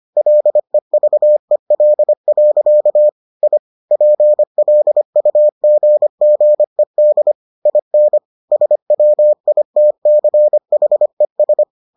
Morse code, heard as a keyed tone, about 25 wpm, 600 Hz. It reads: LEVEL. I PLUGGED IN SWITCHES